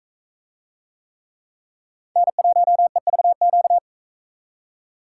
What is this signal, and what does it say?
Morse code, heard as a keyed tone, about 40 wpm, 700 Hz. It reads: N1EVQ